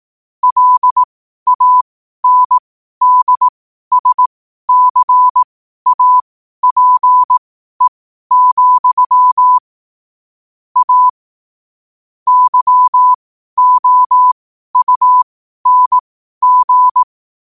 Morse code, heard as a keyed tone, about 18 wpm, spaced as for 12 wpm, 1000 Hz. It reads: LANDSCAPE, A YOUNG
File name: qotdfiles / 12wpm